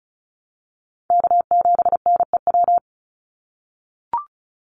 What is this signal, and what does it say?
Morse code, heard as a keyed tone, about 36 wpm, 700 Hz. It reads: K7NEW
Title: NA SOTA Chaser Callsigns 35wpm